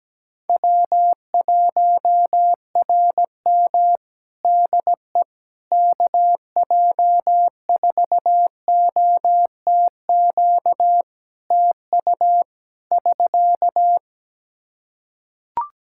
Morse code, heard as a keyed tone, about 17 wpm, 700 Hz. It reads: W1RM DE KJ4OTQ TU <SK>
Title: Practice QSO - Exchange 4 - Ending - 17wpm